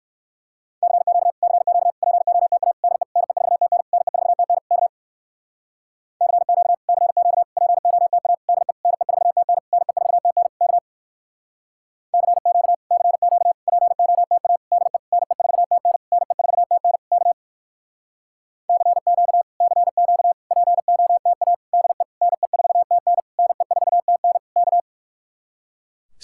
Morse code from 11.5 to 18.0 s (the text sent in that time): CQ CQ POTA DE NE4TN NE4TN K